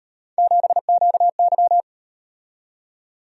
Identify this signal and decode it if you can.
Morse code, heard as a keyed tone, about 38 wpm, 700 Hz. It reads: 7QY